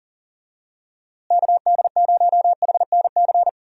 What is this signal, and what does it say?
Morse code, heard as a keyed tone, about 37 wpm, 700 Hz. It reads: KD0HNC
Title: USA Call Signs - 6 Characters - Repeated 2x - 40wpm